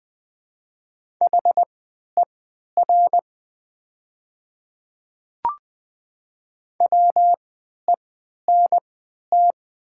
Morse code, heard as a keyed tone, about 20 wpm, spaced as for 11 wpm, 690 Hz. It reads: HER WENT
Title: 3x Character Spacing - Top 300 Words - 20wpm